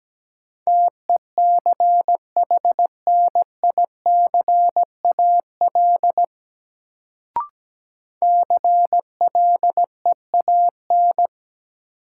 Morse code, heard as a keyed tone, about 17 wpm, 700 Hz. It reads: TECHNICAL CLEAN
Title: Sentences from Top 1100 Words - New Word Drill Alt - 17wpm